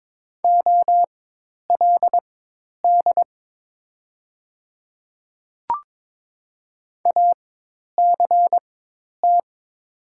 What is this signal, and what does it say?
Morse code, heard as a keyed tone, about 22 wpm, spaced as for 10 wpm, 700 Hz. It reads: OLD ACT
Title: 4x Character Spacing - Top 300 Words - 22wpm